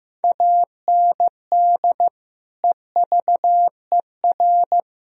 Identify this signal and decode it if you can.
Morse code, heard as a keyed tone, about 16 wpm, 700 Hz. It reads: AND EVER
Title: Sentences from Top 1000 Words - Review Part 1 - 15wpm